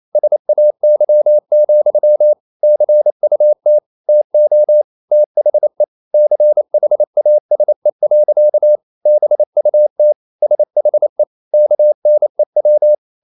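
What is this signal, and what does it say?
Morse code, heard as a keyed tone, about 28 wpm, 600 Hz. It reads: SAY, CUT TO THE CHASE. BUT SHE KNEW
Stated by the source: Morse code practice at 28 wpm, text of Ken Follett's novel Never in English, from CW med Gnister.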